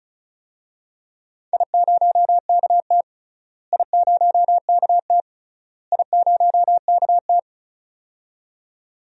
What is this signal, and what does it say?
Morse code, heard as a keyed tone, about 36 wpm, 700 Hz. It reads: I0KT I0KT I0KT
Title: DXCC-100 Call Signs - 4 Characters - Repeated 3x - 35wpm